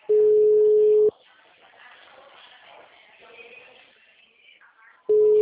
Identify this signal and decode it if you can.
Ringback tone (Europe, 425 Hz).
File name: out- -2006- - - .44966